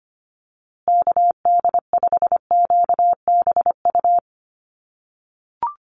Morse code, heard as a keyed tone, about 25 wpm, 700 Hz. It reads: KD5QBU
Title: USA Call Signs Repeated 3x 25wpm